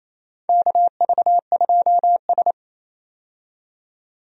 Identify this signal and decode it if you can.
Morse code, heard as a keyed tone, about 27 wpm, 700 Hz. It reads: KV2S
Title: USA Call Signs - 4 Characters - 28wpm